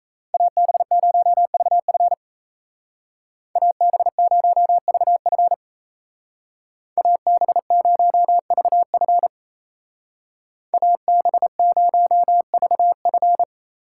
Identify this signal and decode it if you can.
Morse code, heard as a keyed tone, about 33 wpm, 700 Hz. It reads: AB0VF AB0VF AB0VF AB0VF
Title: Speed Racing - USA 2x2 Call Signs 28wpm